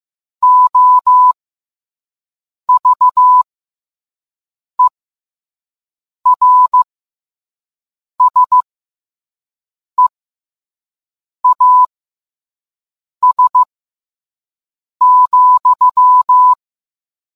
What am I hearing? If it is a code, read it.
Morse code, heard as a keyed tone, about 15 wpm, spaced as for 5 wpm, 1000 Hz. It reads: OVERSEAS,